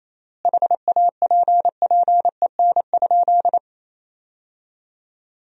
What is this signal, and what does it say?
Morse code, heard as a keyed tone, about 28 wpm, 700 Hz. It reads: HAPPEN?
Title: 3x Word-Spacing - Sentences from Top 400 Words - 28wpm